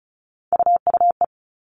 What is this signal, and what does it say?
Morse code, heard as a keyed tone, about 34 wpm, 700 Hz. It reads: UUE
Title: All Cut Numbers - 3-Digit 35wpm